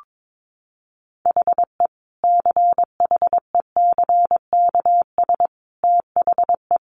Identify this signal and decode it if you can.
Morse code, heard as a keyed tone, about 22 wpm, 700 Hz. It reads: HE CHECKS THE
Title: Easy Phrases 22wpm